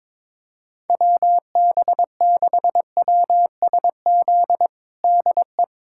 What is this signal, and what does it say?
Morse code, heard as a keyed tone, about 22 wpm, 700 Hz. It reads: WB6WSZ DE
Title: USA Callsign DE Callsign 22wpm